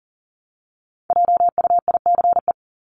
Morse code, heard as a keyed tone, about 40 wpm, 700 Hz. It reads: JUICE
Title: Top 1251-1500 Words 40wpm